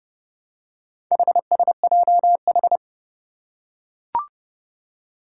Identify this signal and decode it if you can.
Morse code, heard as a keyed tone, about 30 wpm, 700 Hz. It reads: HSJH
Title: Four Challenging Letter-Number Combinations 30wpm